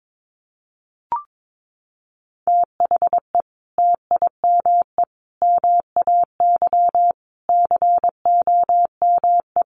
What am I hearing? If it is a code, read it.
Morse code, heard as a keyed tone, about 22 wpm, 700 Hz. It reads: THE TIME MAY COME
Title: Sentences from Top 1100 Words 22wpm